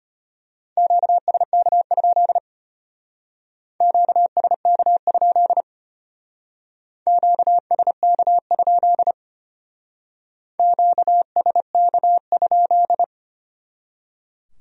Morse code, heard as a keyed tone, about 32 wpm, 700 Hz. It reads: QSK? QSK? QSK? QSK?